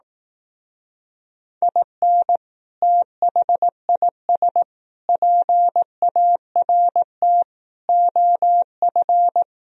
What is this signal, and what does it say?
Morse code, heard as a keyed tone, about 18 wpm, 700 Hz. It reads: IN THIS PART OF
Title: Speed-Racing - Sets of 5 Words 15wpm